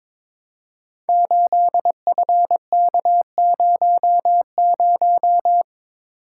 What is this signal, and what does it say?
Morse code, heard as a keyed tone, about 22 wpm, 700 Hz. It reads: 8FK00
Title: Code Groups - 5 Letter:Numbers - 22wpm